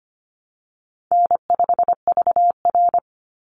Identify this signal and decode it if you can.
Morse code, heard as a keyed tone, about 25 wpm, 700 Hz. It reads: N5VR
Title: Easy Call Signs - 25wpm